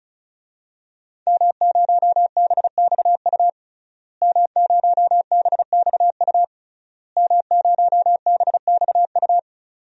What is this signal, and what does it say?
Morse code, heard as a keyed tone, about 36 wpm, 700 Hz. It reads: M0BXU M0BXU M0BXU
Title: DXCC-100 Call Signs - 5 Characters - Repeated 3x - 35wpm